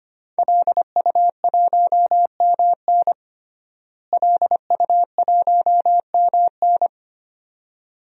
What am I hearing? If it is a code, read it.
Morse code, heard as a keyed tone, about 25 wpm, 700 Hz. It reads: LU1MN LU1MN
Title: DXCC-100 Call Signs - 5 Characters - Repeated 2x - 25wpm